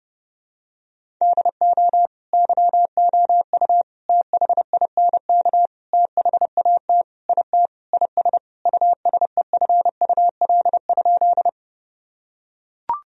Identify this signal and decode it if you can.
Morse code, heard as a keyed tone, about 30 wpm, 700 Hz. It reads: DO YOU THINK THAT IT IS USEFUL?
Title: Sentences from Top 1000 Words - Review Part 4 - 30wpm